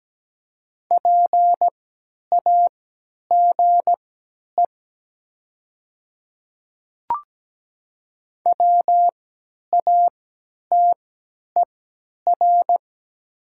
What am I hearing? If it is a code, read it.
Morse code, heard as a keyed tone, about 17 wpm, spaced as for 10 wpm, 700 Hz. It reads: PAGE WATER
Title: 3x Character Spacing - Top 300 Words - 17wpm